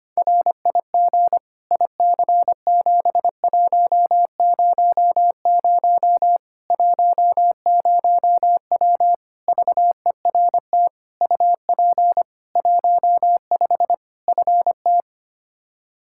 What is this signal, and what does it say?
Morse code, heard as a keyed tone, about 25 wpm, 700 Hz. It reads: RIG IC7100 10W VERT UP 15 FT
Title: Practice QSO - Exchange 2 - Info - 25wpm